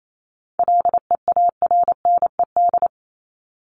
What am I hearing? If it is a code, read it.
Morse code, heard as a keyed tone, about 28 wpm, 700 Hz. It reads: LEARNED
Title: ED Suffix - Top 100 Words with Base Words 28wpm